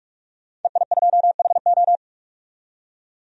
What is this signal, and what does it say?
Morse code, heard as a keyed tone, about 46 wpm, 700 Hz. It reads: EI2HO
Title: DXCC-100 Call Signs - 5 Characters - 45wpm